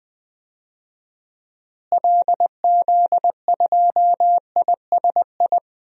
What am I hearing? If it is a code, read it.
Morse code, heard as a keyed tone, about 20 wpm, 700 Hz. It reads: LZ2ISI
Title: DXCC-100 Call Signs - 6 Characters - Repeated 3x - 20wpm